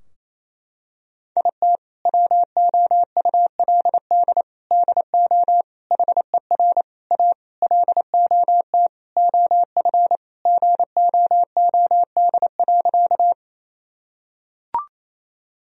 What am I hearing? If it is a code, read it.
Morse code, heard as a keyed tone, about 28 wpm, 700 Hz. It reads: IT WOULD DO HER A LOT OF GOOD.
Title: Sentences from Top 1000 Words - Review Part 3 - 28wpm